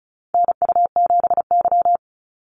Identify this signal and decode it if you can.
Morse code, heard as a keyed tone, about 34 wpm, 700 Hz. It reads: NU7Y
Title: NA SOTA Activator Callsigns 35wpm